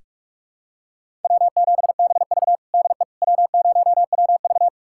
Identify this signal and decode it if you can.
Morse code, heard as a keyed tone, about 46 wpm, 700 Hz. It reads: W7BV DE W0WV
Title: USA Callsign DE Callsign - 4 Characters - 45wpm